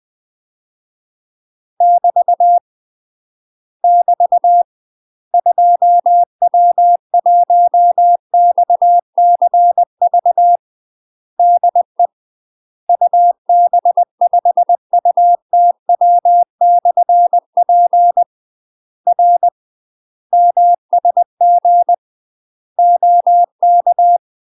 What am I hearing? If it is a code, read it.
Morse code, heard as a keyed tone, about 20 wpm, 700 Hz. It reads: <BT> <BT> 2W1XCV DE UB5UTW/P R MSG OK
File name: UK MASTER QSOs TRAINING/20WPM